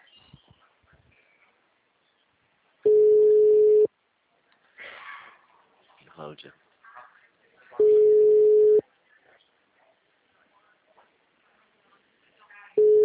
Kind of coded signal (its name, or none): telephone call progress tone